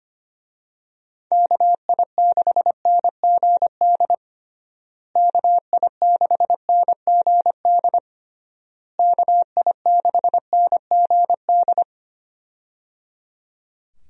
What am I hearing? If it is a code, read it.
Morse code, heard as a keyed tone, about 25 wpm, 700 Hz. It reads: KI6NGD KI6NGD KI6NGD
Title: USA Call Signs - 6 Characters - Repeated 3x - 25wpm